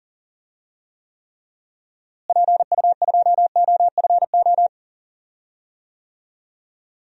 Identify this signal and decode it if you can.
Morse code, heard as a keyed tone, about 38 wpm, 700 Hz. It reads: PU2OFO
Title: DXCC-100 Call Signs - 6 Characters - Repeated 3x - 40wpm